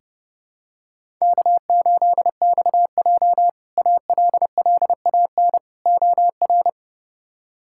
Morse code, heard as a keyed tone, about 30 wpm, 700 Hz. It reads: K8XJ ALLAN OR
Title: K1USN SST QSO Practice - 30wpm